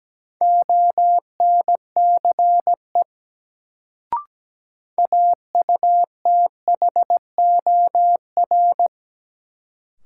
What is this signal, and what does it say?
Morse code, heard as a keyed tone, about 17 wpm, 700 Hz. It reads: ONCE AUTHOR
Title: Sets of 4 Words - New Word Drill - 17wpm